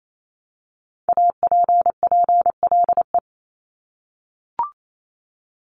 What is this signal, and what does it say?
Morse code, heard as a keyed tone, about 28 wpm, 700 Hz. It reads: APPLE
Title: Multi-letter Phonemes Repeated 3x 28wpm